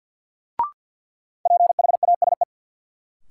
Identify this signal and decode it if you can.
Morse code, heard as a keyed tone, about 52 wpm, 700 Hz. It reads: PHASE
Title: Sentences from Top 1200 Words - New Word Drill Alt - 50wpm